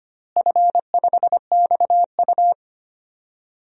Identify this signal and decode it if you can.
Morse code, heard as a keyed tone, about 25 wpm, 700 Hz. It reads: F5XU
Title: DXCC-100 Call Signs - 4 Characters - Repeated 2x - 25wpm